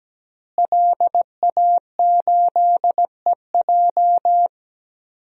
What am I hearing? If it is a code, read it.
Morse code, heard as a keyed tone, about 17 wpm, 700 Hz. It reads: LA8EJ
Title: DXCC-100 Call Signs - 5 Characters - Repeated 2x - 17wpm